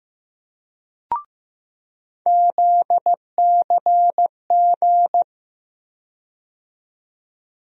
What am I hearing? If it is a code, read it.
Morse code, heard as a keyed tone, about 15 wpm, 700 Hz. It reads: ZCG
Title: USA Call Sign Postfixes - 3 Letters 15wpm